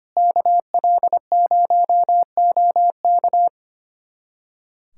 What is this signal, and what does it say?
Morse code, heard as a keyed tone, about 25 wpm, 700 Hz. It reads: KL0OK
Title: USA Callsign DE Callsign - 5 Characters - 25wpm